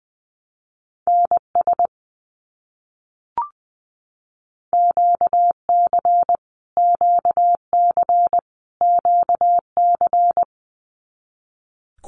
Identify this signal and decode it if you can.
Morse code, heard as a keyed tone, about 20 wpm, 700 Hz. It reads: NS QC QC QC
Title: Canadian Provinces and Territories - Repeated 3x - 20wpm